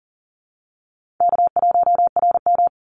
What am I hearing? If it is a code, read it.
Morse code, heard as a keyed tone, about 38 wpm, 700 Hz. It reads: K1RM
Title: Easy Call Signs - 40wpm